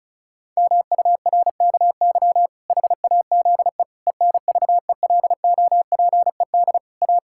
Morse code, heard as a keyed tone, about 36 wpm, 700 Hz. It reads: MURKY HAZE ENVELOPED A
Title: Pangram Challenge v2 - 35wpm